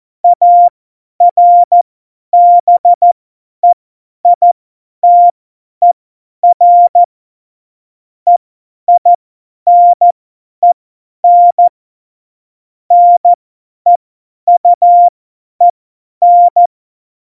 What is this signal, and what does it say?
Morse code, heard as a keyed tone, about 14 wpm, spaced as for 10 wpm, 700 Hz. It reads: ARBEITER EINEN NEUEN
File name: Morsekurs / Witze / Witz 05. Juni 2018, 50 F70